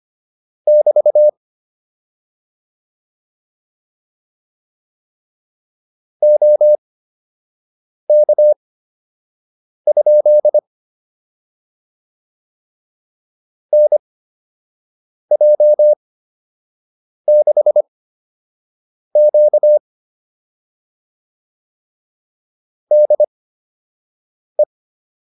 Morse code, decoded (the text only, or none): <BT> OK? NJ6Q DE